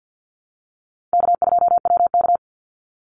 Morse code, heard as a keyed tone, about 50 wpm, 700 Hz. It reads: K2WK
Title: Easy Call Signs Repeated 3x - 50wpm